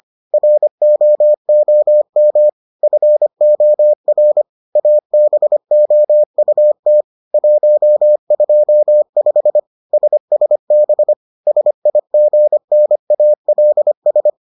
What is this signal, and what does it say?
Morse code, heard as a keyed tone, about 25 wpm, 600 Hz. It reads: ROOM FOR ABOUT 125 SSB SIGNALS